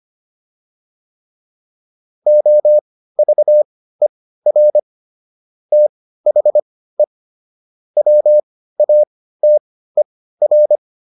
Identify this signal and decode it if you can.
Morse code, heard as a keyed tone, about 25 wpm, spaced as for 15 wpm, 600 Hz. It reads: OVER THE WATER